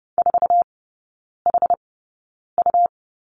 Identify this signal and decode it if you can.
Morse code, heard as a keyed tone, about 30 wpm, spaced as for 9 wpm, 690 Hz. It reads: 4HU
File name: Lesson 17e - Letter u - Three Characters 30wpm